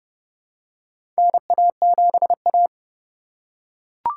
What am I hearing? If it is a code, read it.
Morse code, heard as a keyed tone, about 30 wpm, 700 Hz. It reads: NA7A
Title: USA Call Signs - 4 Characters - 4x Character Spacing - 30wpm